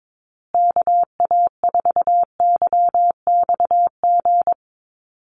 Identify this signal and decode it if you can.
Morse code, heard as a keyed tone, about 22 wpm, 700 Hz. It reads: KA4YXG
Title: USA Call Signs Repeated 2x 22wpm